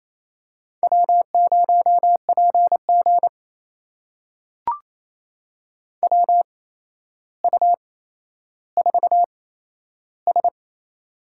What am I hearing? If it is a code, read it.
Morse code, heard as a keyed tone, about 28 wpm, spaced as for 12 wpm, 700 Hz. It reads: W0PG WU4S
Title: USA Call Signs - 8x Character Spacing - 28wpm